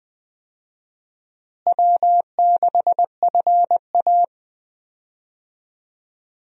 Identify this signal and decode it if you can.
Morse code, heard as a keyed tone, about 20 wpm, 700 Hz. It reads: W6FA
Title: USA Call Signs - 4 Characters - Repeated 3x - 20wpm